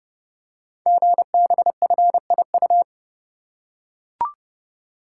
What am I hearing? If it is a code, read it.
Morse code, heard as a keyed tone, about 30 wpm, 700 Hz. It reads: GBFIU
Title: Code Groups - 5 Letters - 30wpm